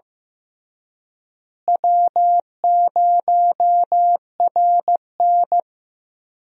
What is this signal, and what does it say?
Morse code, heard as a keyed tone, about 14 wpm, 700 Hz. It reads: W0RN
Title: USA Call Signs - 4 Characters - Repeated 3x - 15wpm